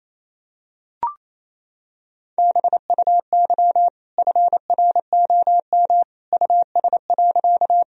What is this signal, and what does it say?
Morse code, heard as a keyed tone, about 28 wpm, 700 Hz. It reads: BUY FROM US.
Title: Sentences from Top 1000 Words - Review Part 4 - 28wpm